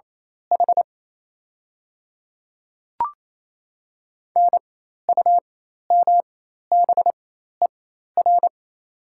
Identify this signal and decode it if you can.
Morse code, heard as a keyed tone, about 28 wpm, spaced as for 13 wpm, 700 Hz. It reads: H NUMBER
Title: 4x Character Spacing - Top 300 Words - 28wpm